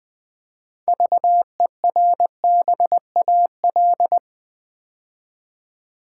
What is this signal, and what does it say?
Morse code, heard as a keyed tone, about 20 wpm, 700 Hz. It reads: VERBAL
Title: 6 Letter Words - First 3 Letters Given - 20wpm